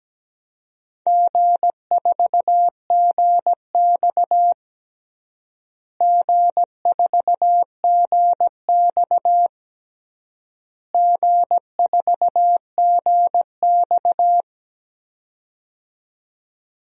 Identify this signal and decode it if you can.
Morse code, heard as a keyed tone, about 17 wpm, 700 Hz. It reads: G4GX G4GX G4GX
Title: DXCC-100 Call Signs - 4 Characters - Repeated 3x - 17wpm